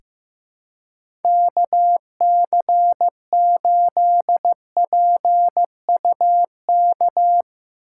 Morse code, heard as a keyed tone, about 15 wpm, 700 Hz. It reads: KC8PUK
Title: USA Call Signs - 6 Characters - 4x Character Spacing - 15wpm